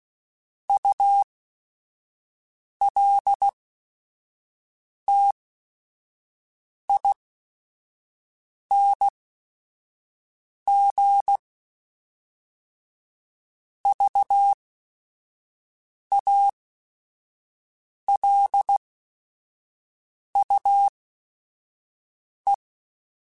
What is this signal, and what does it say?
Morse code, heard as a keyed tone, about 16 wpm, spaced as for 5 wpm, 800 Hz. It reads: ULTING VALUE
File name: Technical Subjects / Technical Subjects-mp3s / XC @5